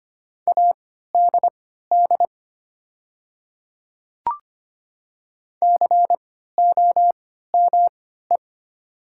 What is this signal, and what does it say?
Morse code, heard as a keyed tone, about 25 wpm, spaced as for 14 wpm, 700 Hz. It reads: ADD COME
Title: 3x Character Spacing - Top 200 Words - 25wpm